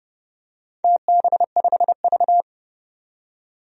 Morse code, heard as a keyed tone, about 30 wpm, 700 Hz. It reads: TB5V
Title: DXCC-100 Call Signs - 4 Characters - 30wpm